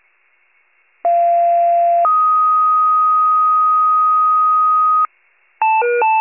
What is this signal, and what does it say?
Two-tone page, heard as a one-second tone, then a three-second tone, about 690 then 1250 Hz. It is heard from 1.0 to 5.1 s.